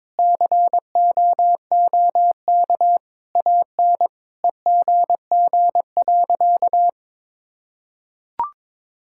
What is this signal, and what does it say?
Morse code, heard as a keyed tone, about 22 wpm, 700 Hz. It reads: COOK AN EGG.
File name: Sentences with 7-Word Phrases - 22wpm